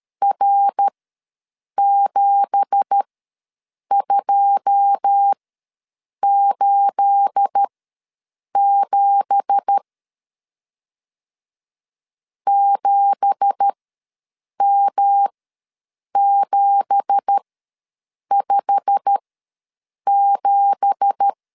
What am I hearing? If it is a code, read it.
Morse code, heard as a keyed tone, about 13 wpm, spaced as for 7 wpm, 800 Hz. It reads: R7287 7M757
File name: LEZIONI / 9 2867 ws 12wpm 3431